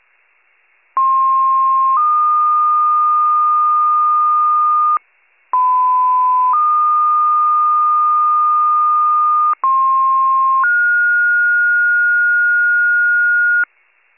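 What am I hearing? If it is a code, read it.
Two-tone page, heard as a one-second tone, then a three-second tone, about 1060 then 1220 Hz, about 1010 then 1220 Hz, about 1060 then 1430 Hz, with noise. It is heard from 1.0 to 13.6 s.
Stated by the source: Fire pager tones from a county-wide paging archive (wcalert).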